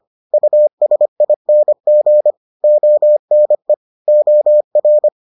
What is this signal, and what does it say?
Morse code, heard as a keyed tone, about 25 wpm, 600 Hz. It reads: USING ONE OR